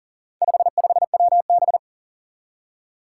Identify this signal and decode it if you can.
Morse code, heard as a keyed tone, about 40 wpm, 700 Hz. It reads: 55WB
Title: Four Challenging Letter-Number Combinations 40wpm